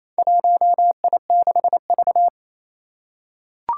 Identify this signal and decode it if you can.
Morse code, heard as a keyed tone, about 28 wpm, 700 Hz. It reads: 1I6V
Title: Four Challenging Letter-Number Combinations 28wpm